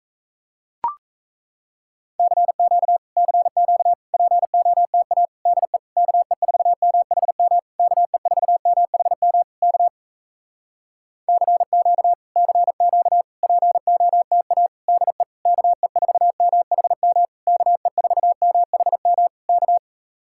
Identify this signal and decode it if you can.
Morse code, heard as a keyed tone, about 37 wpm, 700 Hz. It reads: CQ CQ POTA DE KE4MHM KE4MHM K CQ CQ POTA DE KE4MHM KE4MHM K